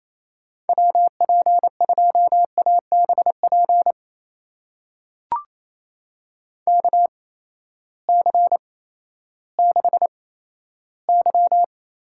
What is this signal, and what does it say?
Morse code, heard as a keyed tone, about 28 wpm, spaced as for 14 wpm, 700 Hz. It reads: WP2ABP KC6Y
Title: USA Call Signs - 6 Characters - 8x Character Spacing - 28wpm